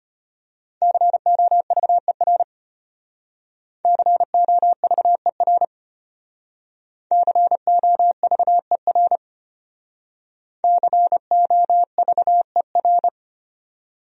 Morse code, heard as a keyed tone, about 33 wpm, 700 Hz. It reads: COVER COVER COVER COVER